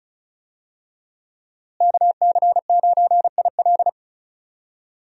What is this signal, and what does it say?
Morse code, heard as a keyed tone, about 34 wpm, 700 Hz. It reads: KC9IL
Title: USA Call Signs - 5 Characters - Repeated 2x - 35wpm